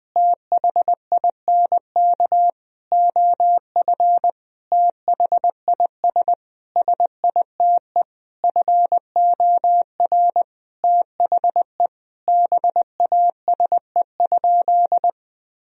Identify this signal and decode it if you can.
Morse code, heard as a keyed tone, about 20 wpm, 700 Hz. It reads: THINK OF THIS SITE FOR THE BASE?
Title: Sentences from Top 1000 Words - Review Part 1 - 20wpm